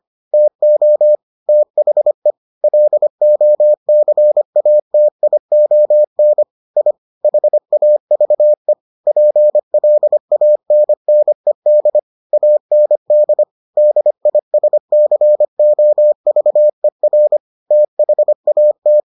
Morse code, heard as a keyed tone, about 25 wpm, 600 Hz. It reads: TO THE LOCATION I HAVE PLANNED AND DISCOVER THAT